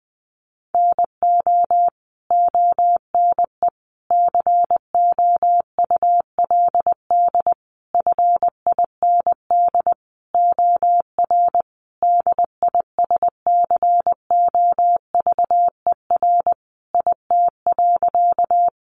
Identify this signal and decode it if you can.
Morse code, heard as a keyed tone, about 20 wpm, 700 Hz. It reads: NO ONE COULD FIND OR DISCOVER IT.
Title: Sentences from Top 1100 Words 20wpm